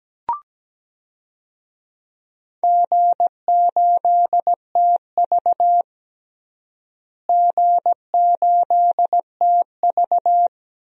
Morse code, heard as a keyed tone, about 17 wpm, 700 Hz. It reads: G8TV G8TV
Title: DXCC-100 Call Signs - 4 Characters - Repeated 2x - 17wpm